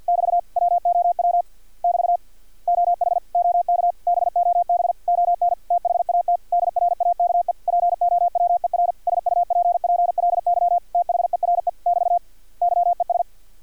Morse code, heard as a keyed tone, about 50 wpm, 700 Hz. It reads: <BT> WOW <BT> OH OK BOB ON THAT DRAKE POWER SUPPLY THERE <BT> YES
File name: archived / CRSnet